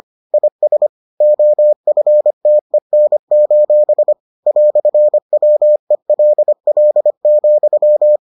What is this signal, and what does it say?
Morse code, heard as a keyed tone, about 25 wpm, 600 Hz. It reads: IS OFTEN: "WELL,